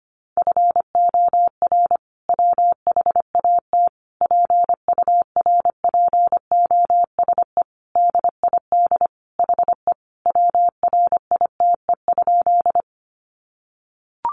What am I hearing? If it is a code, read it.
Morse code, heard as a keyed tone, about 25 wpm, 700 Hz. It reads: FOR WHAT PURPOSE DID HE WRITE?
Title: Sentences from Top 800 Words 25wpm